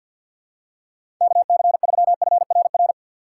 Morse code, heard as a keyed tone, about 51 wpm, 700 Hz. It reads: KC3FRR
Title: USA Call Signs - 2x Character Spacing - 50wpm